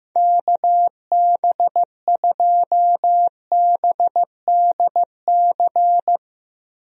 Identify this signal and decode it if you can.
Morse code, heard as a keyed tone, about 15 wpm, 700 Hz. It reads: KB2BDC
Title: USA Call Signs - 6 Characters - Repeated 2x - 15wpm